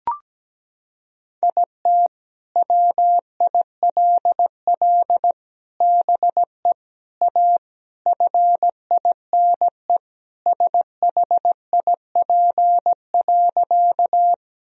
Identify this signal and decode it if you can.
Morse code, heard as a keyed tone, about 17 wpm, 700 Hz. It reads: IT WILL BE A FINE SHIP.
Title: Sentences from Top 1000 Words - Review Part 2 - 17wpm